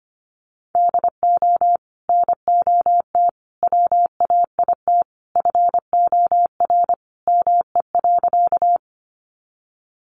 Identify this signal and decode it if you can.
Morse code, heard as a keyed tone, about 25 wpm, 700 Hz. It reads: DO NOT WAIT FOR ME.
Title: Sentences from Top 600 Words 25wpm